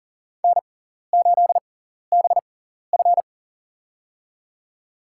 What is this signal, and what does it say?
Morse code, heard as a keyed tone, about 40 wpm, spaced as for 14 wpm, 700 Hz. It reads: N8BF
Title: USA Call Signs - 4 Characters - 6x Character Spacing - 40wpm